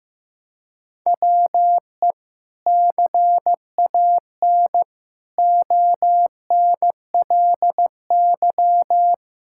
Morse code, heard as a keyed tone, about 15 wpm, 700 Hz. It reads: WE CAN ONLY
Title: Sentences from Top 1000 Words - Review Part 5 - 15wpm